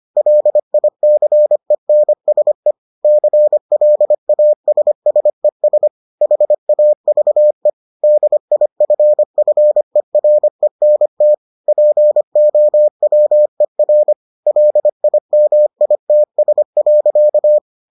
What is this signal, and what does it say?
Morse code, heard as a keyed tone, about 25 wpm, 600 Hz. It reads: LICENSE CLASSES HAVE DIFFERENT POWER LIMITS.